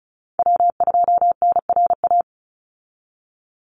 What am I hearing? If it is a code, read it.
Morse code, heard as a keyed tone, about 34 wpm, 700 Hz. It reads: W2NRA
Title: Call Signs Repeated 3x - 35wpm